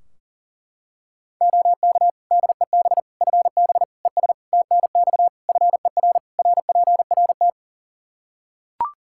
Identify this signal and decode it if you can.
Morse code, heard as a keyed tone, about 40 wpm, 700 Hz. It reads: OK DEB FB ES TNX FER RPRT
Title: Practice QSO - Exchange 2 - Info - 40wpm